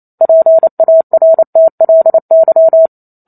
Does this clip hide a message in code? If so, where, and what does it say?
Yes: Morse code from 0.2 to 2.9 s, heard as a keyed tone, about 28 wpm, 650 Hz. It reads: PARTLY